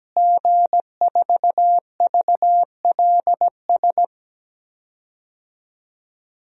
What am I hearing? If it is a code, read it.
Morse code, heard as a keyed tone, about 17 wpm, 700 Hz. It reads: G4VLS
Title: DXCC-100 Call Signs - 5 Characters - Repeated 2x - 17wpm